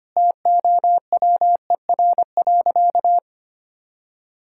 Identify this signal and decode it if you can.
Morse code, heard as a keyed tone, about 25 wpm, 700 Hz. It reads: TOWER.
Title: Pangram Challenge v2 - 25wpm